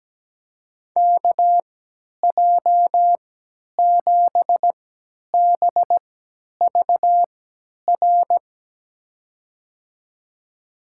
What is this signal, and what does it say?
Morse code, heard as a keyed tone, about 17 wpm, spaced as for 10 wpm, 700 Hz. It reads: KJ7BVR
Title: USA Call Signs - 6 Characters - 3x Character Spacing - 17wpm